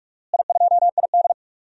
Morse code, heard as a keyed tone, about 41 wpm, 700 Hz. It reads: I2ID
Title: DXCC-100 Call Signs - 4 Characters - Repeated 2x - 45wpm